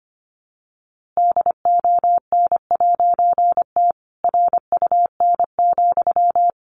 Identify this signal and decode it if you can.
Morse code, heard as a keyed tone, about 25 wpm, 700 Hz. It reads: DON'T RUN,
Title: A Sound Of Thunder 25wpm